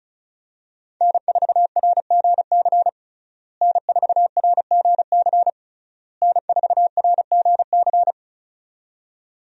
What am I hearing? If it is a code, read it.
Morse code, heard as a keyed tone, about 36 wpm, 700 Hz. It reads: N4RGC N4RGC N4RGC
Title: USA Call Signs - 5 Characters - Repeated 3x - 35wpm